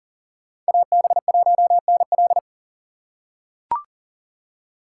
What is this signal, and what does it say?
Morse code, heard as a keyed tone, about 40 wpm, 700 Hz. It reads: AB1NL
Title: USA Call Signs - 5 Characters - 4x Character Spacing - 40wpm